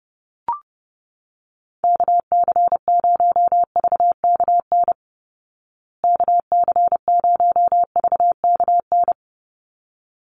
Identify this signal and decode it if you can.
Morse code, heard as a keyed tone, about 30 wpm, 700 Hz. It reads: KC0VKN KC0VKN
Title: Call Signs Repeated 3x - 30wpm